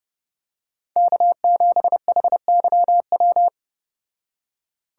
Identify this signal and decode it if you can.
Morse code, heard as a keyed tone, about 30 wpm, 700 Hz. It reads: K7HYW